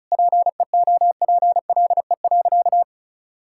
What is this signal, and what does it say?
Morse code, heard as a keyed tone, about 36 wpm, 700 Hz. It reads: PEOPLE.
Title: Sentences from Top 1000 Words - Review Part 5 - 35wpm